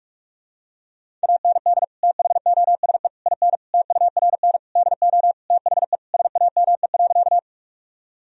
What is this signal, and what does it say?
Morse code, heard as a keyed tone, about 46 wpm, 700 Hz. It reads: AND THOSE IN TURN DO THE SAME.
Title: Sentences from Top 1000 Words - Review Part 1 - 45wpm